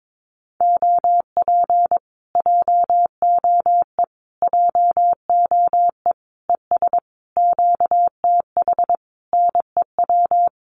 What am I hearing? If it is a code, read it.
Morse code, heard as a keyed tone, about 22 wpm, 700 Hz. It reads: OP JOE JOE ES QTH NEW
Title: Common QSO Phrases Encore 22wpm